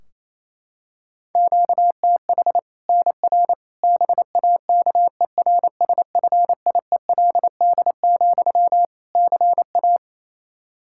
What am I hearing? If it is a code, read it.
Morse code, heard as a keyed tone, about 28 wpm, 700 Hz. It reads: QTH NR BAKERSFIELD, CA
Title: K1USN SST QSO Practice 28wpm